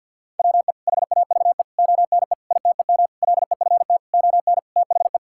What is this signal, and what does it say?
Morse code, heard as a keyed tone, about 49 wpm, 700 Hz. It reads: WE HAVE ONE ITEM LEFT ON THE
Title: Sentences from Top 1000 Words - Review Part 6 - 50wpm